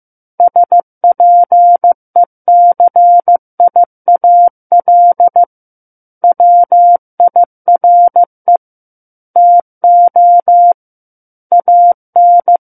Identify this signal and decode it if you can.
Morse code, heard as a keyed tone, about 15 wpm, 700 Hz. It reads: SPECIAL WIRE TO AN
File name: Book One / 06-Chapter 6 - 15 WPM